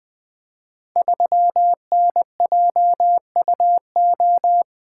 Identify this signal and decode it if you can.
Morse code, heard as a keyed tone, about 20 wpm, 700 Hz. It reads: 3NJUO
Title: Code Groups - 5 Letter:Numbers - 20wpm